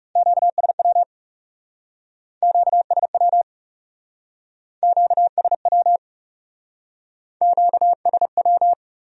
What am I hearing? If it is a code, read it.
Morse code, heard as a keyed tone, about 37 wpm, 700 Hz. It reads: QSW QSW QSW QSW
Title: Speed-Racing Q-Codes - 30wpm